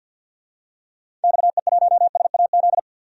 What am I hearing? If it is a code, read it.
Morse code, heard as a keyed tone, about 50 wpm, 700 Hz. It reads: XE1SAZ
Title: DXCC-100 Call Signs - 6 Characters - 50wpm